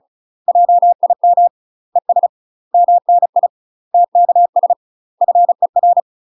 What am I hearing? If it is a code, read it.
Morse code, heard as a keyed tone, about 34 wpm, 700 Hz. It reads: JIM ES MNI TKS FER
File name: UK MASTER QSOs TRAINING / 35WPM / UK MASTER QSOs 0024